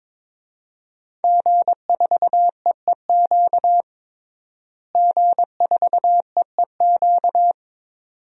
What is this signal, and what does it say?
Morse code, heard as a keyed tone, about 22 wpm, 700 Hz. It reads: G4EEQ G4EEQ
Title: DXCC-100 Call Signs - 5 Characters - Repeated 2x - 22wpm